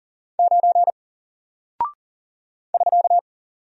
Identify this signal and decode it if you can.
Morse code, heard as a keyed tone, about 40 wpm, 700 Hz. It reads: 9 <SK>
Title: Single Character (Letter:Number:Punctuation:Prosign) Alt 40wpm